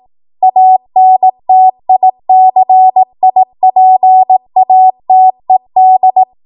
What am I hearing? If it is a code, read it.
Morse code, heard as a keyed tone, about 18 wpm, 750 Hz. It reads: ANTICIPATED
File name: Morse / 18WPM